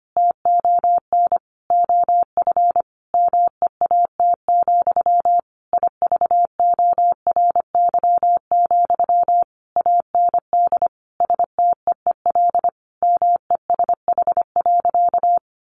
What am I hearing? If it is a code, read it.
Morse code, heard as a keyed tone, about 25 wpm, 700 Hz. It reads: TON OF MEAT, IVORY, AND STEEL MESH.
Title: A Sound Of Thunder 25wpm